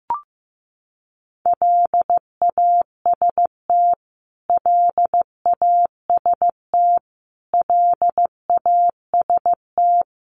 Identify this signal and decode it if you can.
Morse code, heard as a keyed tone, about 15 wpm, 700 Hz. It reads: LAST LAST LAST
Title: Top 500 Words Repeated 15wpm